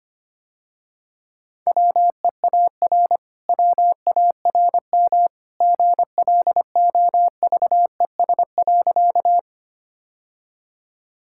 Morse code, heard as a keyed tone, about 25 wpm, 700 Hz. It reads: WEAR WARM GLOVES.
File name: Easy Sentences with Hints - 25wpm